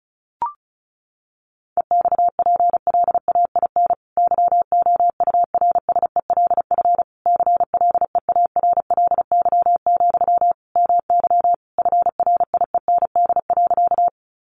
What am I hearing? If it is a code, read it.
Morse code, heard as a keyed tone, about 34 wpm, 700 Hz. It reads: EXPLAIN YOURSELF CLEARLY, MY FRIEND.
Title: Sentences from Top 1300 Words 35wpm